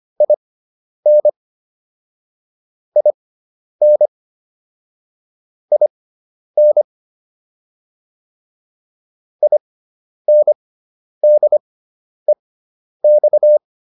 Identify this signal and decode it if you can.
Morse code, heard as a keyed tone, about 25 wpm, spaced as for 10 wpm, 600 Hz. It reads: IN IN IN INDEX